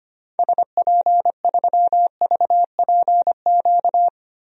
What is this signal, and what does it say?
Morse code, heard as a keyed tone, about 25 wpm, 700 Hz. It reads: SP3VPQ
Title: DXCC-100 Call Signs - 6 Characters - 25wpm